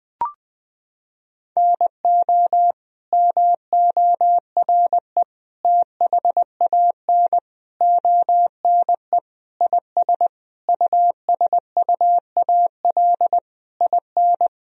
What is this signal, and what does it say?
Morse code, heard as a keyed tone, about 21 wpm, 700 Hz. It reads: NO MORE THAN ONE IS USUAL IN
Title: Sentences from Top 1000 Words - Review Part 2 - 20wpm